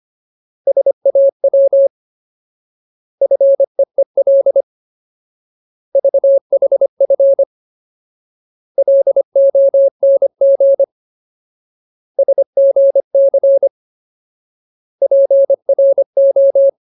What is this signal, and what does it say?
Morse code, heard as a keyed tone, about 25 wpm, 550 Hz. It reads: SAW FEEL VHF LONG SGC PRO
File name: Ultimate Word List / 25WPM Ultimate